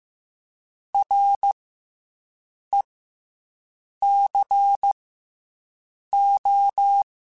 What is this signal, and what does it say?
Morse code, heard as a keyed tone, about 15 wpm, spaced as for 6 wpm, 800 Hz. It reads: RECO